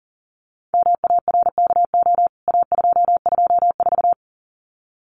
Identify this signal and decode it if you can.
Morse code, heard as a keyed tone, about 40 wpm, 700 Hz. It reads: MARKO A224
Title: CWT Exchange 40wpm